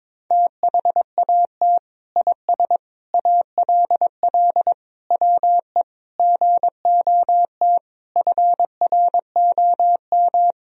Morse code, heard as a keyed tone, about 22 wpm, 700 Hz. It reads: THAT IS ALL WE GOT FROM
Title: Sentences from Top 1000 Words - Review Part 2 - 22wpm